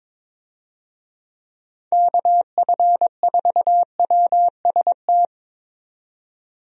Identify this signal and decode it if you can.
Morse code, heard as a keyed tone, about 22 wpm, 700 Hz. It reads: KF4WST
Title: USA Call Signs - 6 Characters - 22wpm